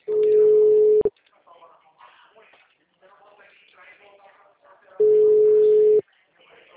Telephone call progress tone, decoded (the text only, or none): ringback tone